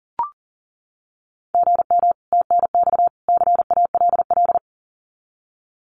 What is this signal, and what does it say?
Morse code, heard as a keyed tone, about 40 wpm, 700 Hz. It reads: GM TNX CALL
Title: Common QSO Phrases 40wpm